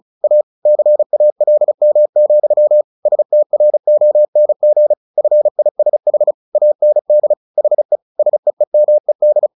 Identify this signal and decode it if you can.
Morse code, heard as a keyed tone, about 34 wpm, 600 Hz. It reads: A CALM, STRONG FISH AND HE SEEMED
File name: the old man and the sea 2 35wpm